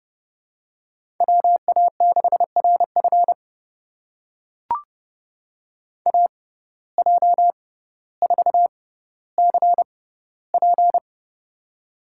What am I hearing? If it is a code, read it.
Morse code, heard as a keyed tone, about 30 wpm, spaced as for 15 wpm, 700 Hz. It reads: WA6RF AJ4CP
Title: USA Call Signs - 5 Characters - 6x Character Spacing - 30wpm